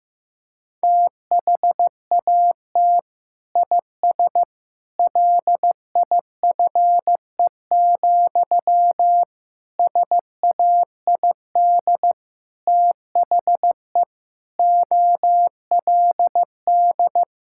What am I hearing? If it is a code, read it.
Morse code, heard as a keyed tone, about 15 wpm, 700 Hz. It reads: THAT IS LIFE, SAID THE OLD
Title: Sentences from Top 1000 Words - Review Part 6 - 15wpm